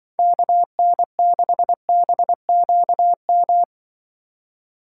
Morse code, heard as a keyed tone, about 24 wpm, 700 Hz. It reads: KN6BQM
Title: Speed Racing - USA 2x3 Call Signs 20wpm